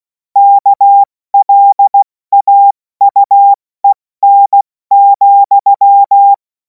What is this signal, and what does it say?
Morse code, heard as a keyed tone, about 16 wpm, spaced as for 14 wpm, 800 Hz. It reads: KLAUEN,
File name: Morsekurs / Witze / Witz 03. März 2026 70 F80